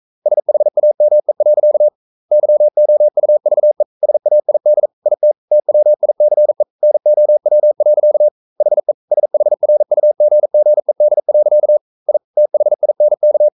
Morse code, heard as a keyed tone, about 42 wpm, 600 Hz. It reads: SHAME. YOUVE SAID IT TWICE NOW. HE SHRUGGED. I THINK
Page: Romanen Never af Ken Follet på engelsk.